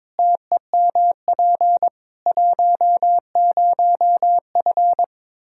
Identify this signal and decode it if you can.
Morse code, heard as a keyed tone, about 22 wpm, 700 Hz. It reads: TEMP 10F
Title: Practice QSO - Exchange 2 - Info - 22wpm